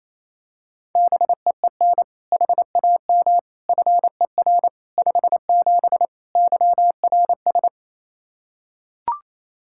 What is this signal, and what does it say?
Morse code, heard as a keyed tone, about 28 wpm, 700 Hz. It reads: BEEN HAM FER 57 YRS
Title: K1USN SST QSO Practice 28wpm